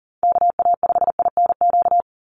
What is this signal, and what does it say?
Morse code, heard as a keyed tone, about 40 wpm, 700 Hz. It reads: KA5INQ
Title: USA Call Signs 40wpm